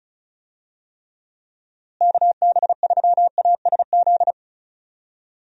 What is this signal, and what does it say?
Morse code, heard as a keyed tone, about 36 wpm, 700 Hz. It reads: KB3ASZ
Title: USA Call Signs - 6 Characters - Repeated 2x - 35wpm